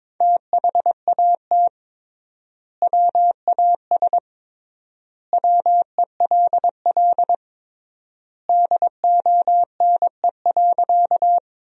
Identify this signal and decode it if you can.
Morse code, heard as a keyed tone, about 22 wpm, 700 Hz. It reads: THAT WAS WELL DONE.
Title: 3x Word-Spacing - Sentences from Top 500 Words - 22wpm